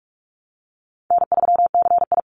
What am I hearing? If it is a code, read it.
Morse code, heard as a keyed tone, about 46 wpm, 700 Hz. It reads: N3CI
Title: Easy Call Signs Repeated 3x - 45wpm